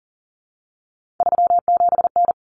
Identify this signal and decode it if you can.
Morse code, heard as a keyed tone, about 40 wpm, 700 Hz. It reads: 37N
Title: Common Cut Numbers - 3-Digit 40wpm